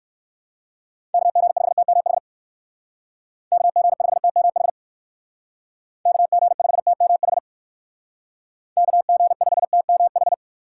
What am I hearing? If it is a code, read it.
Morse code, heard as a keyed tone, about 60 wpm, 700 Hz. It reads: KG5TMH KG5TMH KG5TMH KG5TMH